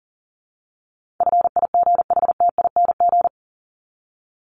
Morse code, heard as a keyed tone, about 40 wpm, 700 Hz. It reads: FIGHTING
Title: ING Suffix - Top 100 Words with Base Words 40wpm